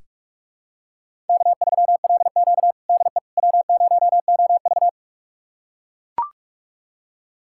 Morse code, heard as a keyed tone, about 41 wpm, 700 Hz. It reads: K3LQ DE W0OV
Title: USA Callsign DE Callsign - 4 Characters - 45wpm